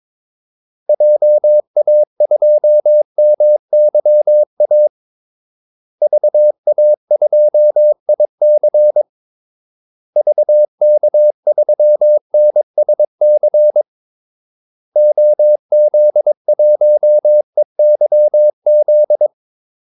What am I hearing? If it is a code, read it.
Morse code, heard as a keyed tone, about 22 wpm, 600 Hz. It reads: JA2MYA VA2IC VK3NSC OZ1EYZ
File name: Callsigns-22wpm